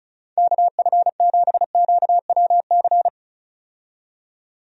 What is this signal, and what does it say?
Morse code, heard as a keyed tone, about 34 wpm, 700 Hz. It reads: KF7QWC
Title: USA Call Signs - 6 Characters - 35wpm